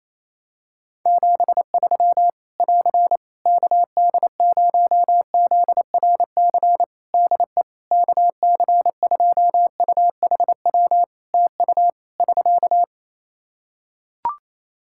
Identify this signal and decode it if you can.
Morse code, heard as a keyed tone, about 27 wpm, 700 Hz. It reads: 73 <AR> KD0ZRC DE KC2UHW TU <SK>
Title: Practice QSO - Exchange 4 - Ending - 28wpm